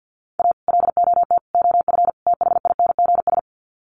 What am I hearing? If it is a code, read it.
Morse code, heard as a keyed tone, about 49 wpm, 700 Hz. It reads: A LOT OF THINGS
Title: Sets of 4 Words 50wpm